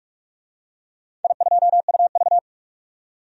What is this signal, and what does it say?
Morse code, heard as a keyed tone, about 44 wpm, 700 Hz. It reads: I2UV
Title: DXCC-100 Call Signs - 4 Characters - 45wpm